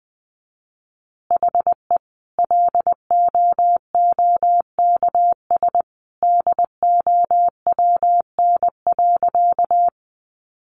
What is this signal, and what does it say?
Morse code, heard as a keyed tone, about 20 wpm, 700 Hz. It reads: HE LOOKS DOWN.
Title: Additional Easy Phrases 20wpm